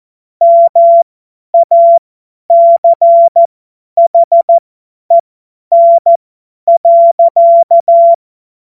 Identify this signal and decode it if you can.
Morse code, heard as a keyed tone, about 14 wpm, spaced as for 10 wpm, 680 Hz. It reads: MACHEN.
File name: Morsekurs / Witze / Witz 19 März 2024 50 F70